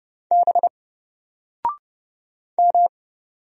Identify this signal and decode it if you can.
Morse code, heard as a keyed tone, about 30 wpm, 700 Hz. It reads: B M
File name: Single-Letter-Number-30wpm